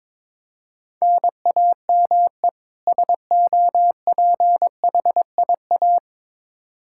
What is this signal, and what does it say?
Morse code, heard as a keyed tone, about 22 wpm, 700 Hz. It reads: NAME SOPHIA
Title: Practice QSO - Exchange 1 - Beginning - 22wpm